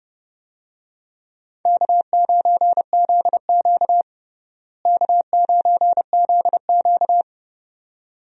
Morse code, heard as a keyed tone, about 30 wpm, 700 Hz. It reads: K9ZQ K9ZQ
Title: USA Call Signs - 4 Characters - Repeated 2x - 30wpm